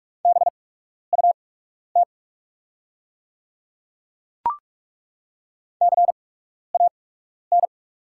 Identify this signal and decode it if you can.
Morse code, heard as a keyed tone, about 43 wpm, spaced as for 12 wpm, 690 Hz. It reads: BUT CAN
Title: 8x Character Spacing - Top 100 Words - 45wpm